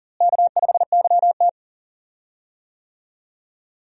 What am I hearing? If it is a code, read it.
Morse code, heard as a keyed tone, about 40 wpm, 700 Hz. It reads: K5YT
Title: USA Call Signs - 4 Characters - Repeated 2x - 40wpm